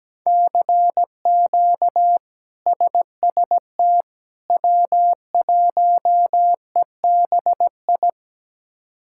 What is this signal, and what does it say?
Morse code, heard as a keyed tone, about 17 wpm, 700 Hz. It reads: CQ SST W1EBI
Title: K1USN SST QSO Practice - 17wpm